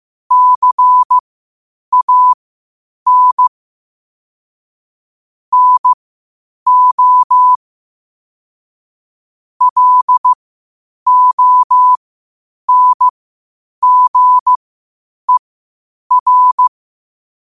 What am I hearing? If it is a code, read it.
Morse code, heard as a keyed tone, about 15 wpm, spaced as for 8 wpm, 1000 Hz. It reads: CAN NO LONGER
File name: 8wpm